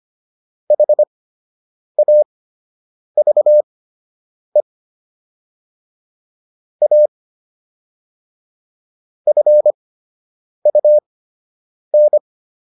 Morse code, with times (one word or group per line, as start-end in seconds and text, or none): 0.7-4.6 HAVE
6.8-7.1 A
9.3-12.2 FUN